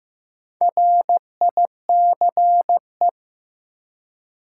Morse code, heard as a keyed tone, about 15 wpm, 700 Hz. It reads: RICE
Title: Four Letter Words - 15wpm